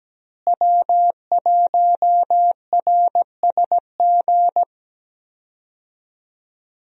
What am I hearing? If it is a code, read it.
Morse code, heard as a keyed tone, about 17 wpm, 700 Hz. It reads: W1RSG
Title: USA Call Signs - 5 Characters - Repeated 3x - 17wpm